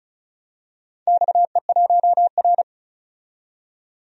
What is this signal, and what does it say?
Morse code, heard as a keyed tone, about 36 wpm, 690 Hz. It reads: XE1R
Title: DXCC-100 Call Signs - 4 Characters - 35wpm